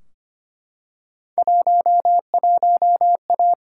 Morse code, heard as a keyed tone, about 25 wpm, 700 Hz. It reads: 11A
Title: Field Day Exchanges Encore - 25wpm